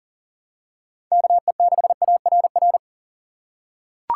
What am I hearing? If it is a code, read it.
Morse code, heard as a keyed tone, about 37 wpm, 700 Hz. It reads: KE6ARR
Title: USA Call Signs - 6 Characters - 3x Character Spacing - 40wpm